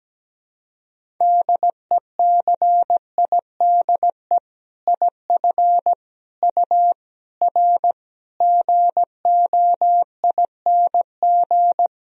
Morse code, heard as a keyed tone, about 17 wpm, 700 Hz. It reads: DECIDE IF U R GOING
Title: Sentences from Top 500 Words and Abbreviations 17wpm